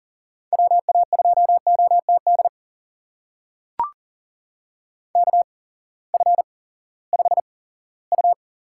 Morse code, heard as a keyed tone, about 37 wpm, spaced as for 19 wpm, 700 Hz. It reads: WA2OTD KF5U
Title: USA Call Signs - 6 Characters - 8x Character Spacing - 40wpm